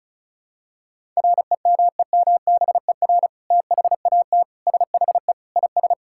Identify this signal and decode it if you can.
Morse code, heard as a keyed tone, about 34 wpm, 700 Hz. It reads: REMEMBER THAT SHE IS
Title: Sentences from Top 1000 Words - Review Part 6 - 35wpm